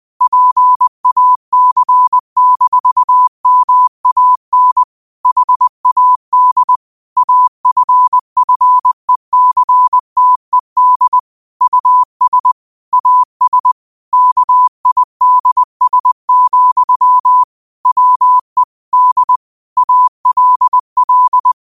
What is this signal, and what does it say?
Morse code, heard as a keyed tone, about 20 wpm, 1000 Hz. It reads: PAC-MAN HAD AFFECTED US AS KIDS, WE D ALL
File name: qotdfiles / 20wpm